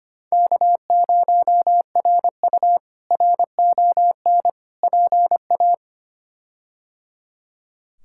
Morse code, heard as a keyed tone, about 25 wpm, 700 Hz. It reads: K0RU RON PA
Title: K1USN SST QSO Practice - 25wpm